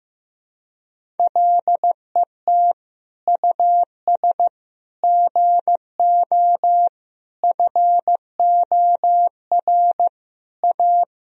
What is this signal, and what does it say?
Morse code, heard as a keyed tone, about 15 wpm, 700 Hz. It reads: LET US GO FOR A
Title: Sentences from Top 1000 Words - Review Part 1 - 15wpm